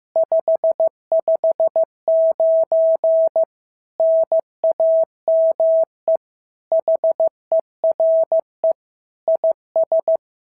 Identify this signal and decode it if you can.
Morse code, heard as a keyed tone, about 14 wpm, 650 Hz. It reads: 559 NAME HERE IS
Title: SOTA QSO Practice 15wpm